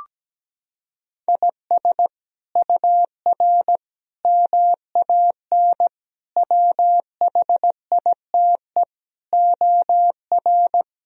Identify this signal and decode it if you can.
Morse code, heard as a keyed tone, about 17 wpm, 700 Hz. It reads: IS UR MAN WHITE OR
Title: Sentences from Top 400 Words and Abbreviations 17wpm